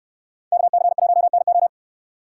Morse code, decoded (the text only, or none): KC1NO